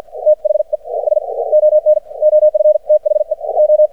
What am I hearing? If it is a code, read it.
Morse code, heard as a keyed tone, about 55 wpm, spaced as for 40 wpm, 610 Hz. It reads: THE SON OF THE O
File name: archived / CRSnet